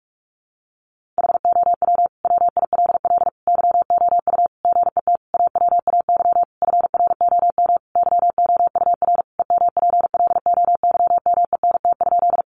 Morse code, heard as a keyed tone, about 44 wpm, 700 Hz. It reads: HOW WILL YOU GET AWAY FROM YOUR EMPLOYMENT?
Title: Sentences from Top 1500 Words 45wpm